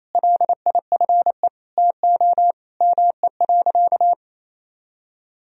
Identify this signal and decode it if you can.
Morse code, heard as a keyed tone, about 28 wpm, 700 Hz. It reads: LIFE TO ME.
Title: Sentences from Top 500 Words and Abbreviations 28wpm